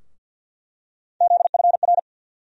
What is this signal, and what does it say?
Morse code, heard as a keyed tone, about 46 wpm, 700 Hz. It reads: ZFR
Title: USA Call Sign Postfixes - 3 Letters 50wpm